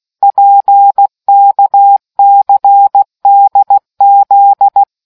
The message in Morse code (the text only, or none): PKCDZ